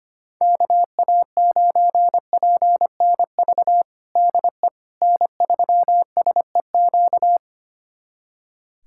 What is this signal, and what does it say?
Morse code, heard as a keyed tone, about 25 wpm, 700 Hz. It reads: KA9PNV DE N3SEQ
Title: USA Callsign DE Callsign 25wpm